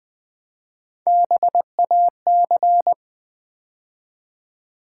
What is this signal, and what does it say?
Morse code, heard as a keyed tone, about 20 wpm, 700 Hz. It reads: BAC
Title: Build A Word - 3 Letters At A Time - 9-Letter Words - 20wpm